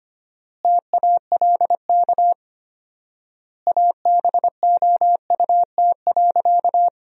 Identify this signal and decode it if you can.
Morse code, heard as a keyed tone, about 25 wpm, 700 Hz. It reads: TALK ABOUT.
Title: 4x Word-Spacing - Sentences from Top 400 Words - 25wpm